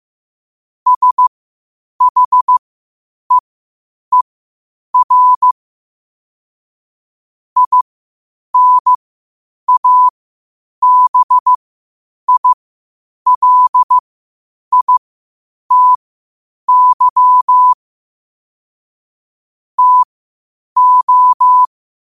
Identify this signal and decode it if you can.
Morse code, heard as a keyed tone, about 15 wpm, spaced as for 8 wpm, 1000 Hz. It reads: SHEER INABILITY TO
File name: qotdfiles / 8wpm